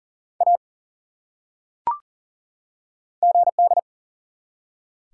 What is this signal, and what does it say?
Morse code, heard as a keyed tone, about 39 wpm, 700 Hz. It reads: A GD
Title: Top 180 QSO Elements 40wpm